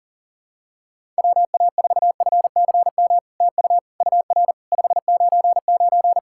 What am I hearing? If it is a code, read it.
Morse code, heard as a keyed tone, about 40 wpm, 700 Hz. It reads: WA4FCM TU UR 599
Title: SOTA QSO Practice 40wpm